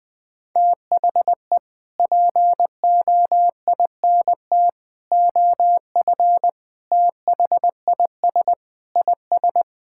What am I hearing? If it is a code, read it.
Morse code, heard as a keyed tone, about 20 wpm, 700 Hz. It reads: THE POINT OF THIS IS
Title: Sentences from Top 1600 Words 20wpm